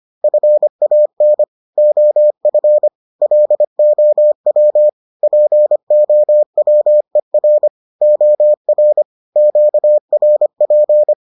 Morse code, heard as a keyed tone, about 25 wpm, 600 Hz. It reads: FAN OF LOW POWER OR QRP